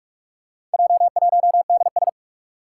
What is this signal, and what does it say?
Morse code, heard as a keyed tone, about 44 wpm, 700 Hz. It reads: J1DS
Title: Four Challenging Letter-Number Combinations 45wpm